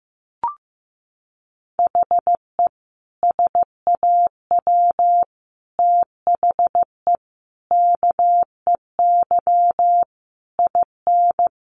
Morse code, heard as a keyed tone, about 15 wpm, 700 Hz. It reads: HE SAW THE KEY IN
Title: Sentences from Top 1300 Words 15wpm